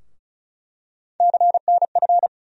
Morse code, heard as a keyed tone, about 34 wpm, 700 Hz. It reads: CNF
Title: USA Call Sign Postfixes - 3 Letters 35wpm